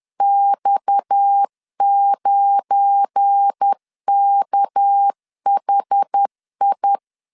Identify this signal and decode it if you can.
Morse code, heard as a keyed tone, about 10 wpm, 800 Hz. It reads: X9KHI
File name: LEZIONI / 7 GF9Z / 7 GF9 10wpm 2747